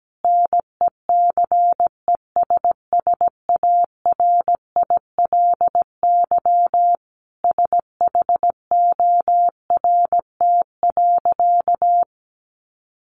Morse code, heard as a keyed tone, about 17 wpm, 700 Hz. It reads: NECESSARILY SHORT.
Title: Sentences from Top 1400 Words 17wpm